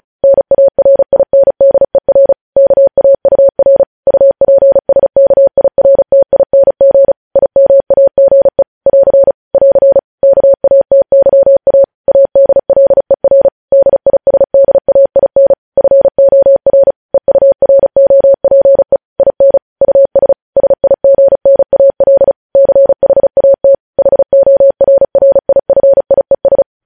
Morse code, heard as a keyed tone, about 36 wpm, 550 Hz. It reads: NARINDER KAUR UPSKIRTING IMAGE <AR> <AR> KATYA ADLER DISDAIN FOR EUROPE IN US SIGNAL CHAT HORRIFIES
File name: podcast / News headlines-1x-35x35